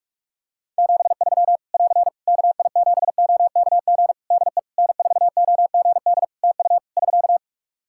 Morse code, heard as a keyed tone, about 46 wpm, 700 Hz. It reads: 73 <AR> KI7OKG DE N4OGD TU <SK>
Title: K1USN SST QSO Practice 45wpm